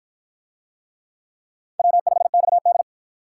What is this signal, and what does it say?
Morse code, heard as a keyed tone, about 50 wpm, 700 Hz. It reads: W5XD